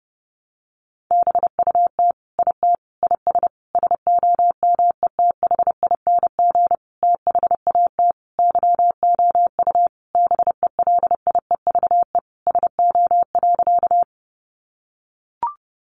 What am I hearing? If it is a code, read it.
Morse code, heard as a keyed tone, about 30 wpm, 700 Hz. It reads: BUT IT IS SOMETHING THAT YOU BELIEVE SO.
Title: Sentences from Top 1000 Words 30wpm